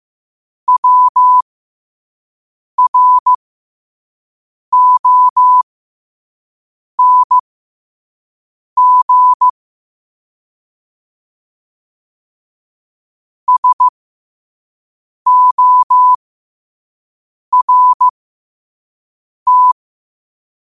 Morse code, heard as a keyed tone, about 15 wpm, spaced as for 5 wpm, 1000 Hz. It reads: WRONG SORT
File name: qotdfiles / 5wpm